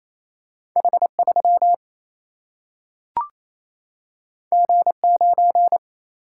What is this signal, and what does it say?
Morse code, heard as a keyed tone, about 28 wpm, 700 Hz. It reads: H 3 G 9
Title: Single-Letter Single-Number 28wpm